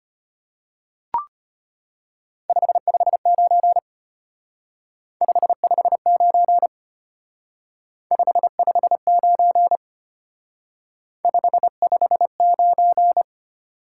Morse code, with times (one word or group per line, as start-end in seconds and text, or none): 2.5-3.8 559
5.2-6.7 559
8.1-9.8 559
11.2-13.2 559